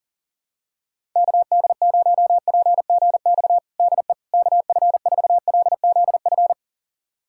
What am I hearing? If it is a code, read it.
Morse code, heard as a keyed tone, about 40 wpm, 700 Hz. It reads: KD0PGX DE KF4LZF
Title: USA Callsign DE Callsign 40wpm